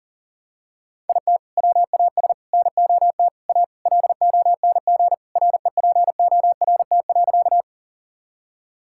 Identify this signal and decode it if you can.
Morse code, heard as a keyed tone, about 40 wpm, 700 Hz. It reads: IT WAS NOT A LONG REPORT.
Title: Sentences from Top 1000 Words - Review Part 6 - 40wpm